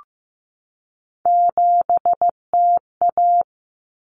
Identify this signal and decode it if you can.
Morse code, heard as a keyed tone, about 15 wpm, 700 Hz. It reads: 7TA
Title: Common Cut Numbers - 3-Digit 15wpm